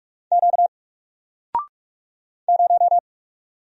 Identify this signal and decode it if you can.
Morse code, heard as a keyed tone, about 45 wpm, 700 Hz. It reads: Q 0
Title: Single Letter-Number 45wpm